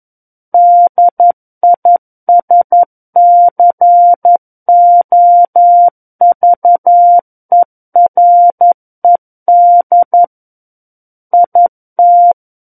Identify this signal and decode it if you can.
Morse code, heard as a keyed tone, about 11 wpm, 700 Hz. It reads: DISCOVERED IT
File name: 11 WPM